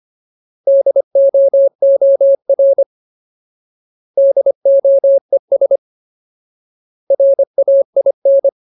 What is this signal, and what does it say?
Morse code, heard as a keyed tone, about 25 wpm, 550 Hz. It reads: DOOR DOES RAIN
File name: Ultimate Word List / 25WPM Ultimate